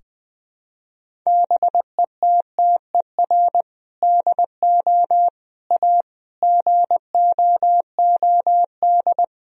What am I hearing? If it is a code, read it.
Morse code, heard as a keyed tone, about 21 wpm, 700 Hz. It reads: BETTER DO A GOOD
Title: Sentences from Top 1000 Words - Review Part 3 - 20wpm